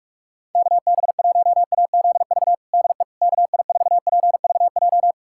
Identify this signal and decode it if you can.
Morse code, heard as a keyed tone, about 46 wpm, 700 Hz. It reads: KB1AZV DE KI4PVJ
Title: USA Callsign DE Callsign - 6 Characters - 45wpm